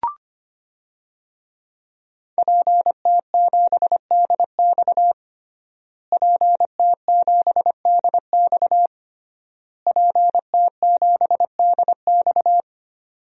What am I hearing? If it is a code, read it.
Morse code, heard as a keyed tone, about 25 wpm, 700 Hz. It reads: PT7DX PT7DX PT7DX
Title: DXCC-100 Call Signs - 5 Characters - Repeated 3x - 25wpm